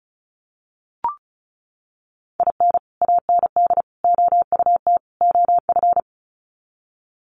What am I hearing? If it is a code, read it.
Morse code, heard as a keyed tone, about 34 wpm, 700 Hz. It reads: IN AND OUT OF
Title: Sets of 4 Words 35wpm